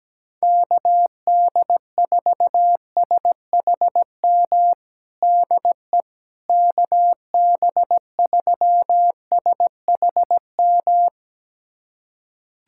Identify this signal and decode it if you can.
Morse code, heard as a keyed tone, about 17 wpm, 700 Hz. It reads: KD4SHM DE KB3SHM
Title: USA Callsign DE Callsign - 6 Characters - 17wpm